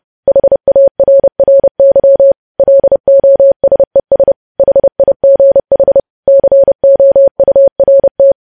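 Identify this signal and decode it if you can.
Morse code, heard as a keyed tone, about 30 wpm, 550 Hz. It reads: HARRY LOSES HIGH COURT
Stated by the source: News headlines sent in Morse code at 30.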